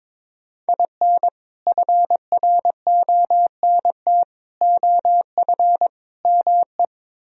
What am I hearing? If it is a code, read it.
Morse code, heard as a keyed tone, about 22 wpm, 700 Hz. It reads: IN FRONT OF ME
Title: Speed-Racing - Sets of 4 Words 22wpm